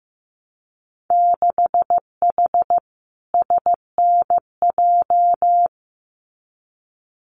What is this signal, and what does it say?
Morse code, heard as a keyed tone, about 15 wpm, 700 Hz. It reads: 6H SNJ
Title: Winter Field Day Exchanges Encore 15wpm